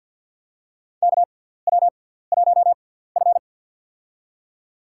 Morse code, heard as a keyed tone, about 54 wpm, spaced as for 17 wpm, 700 Hz. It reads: KW1F
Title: USA Call Signs - 4 Characters - 6x Character Spacing - 50wpm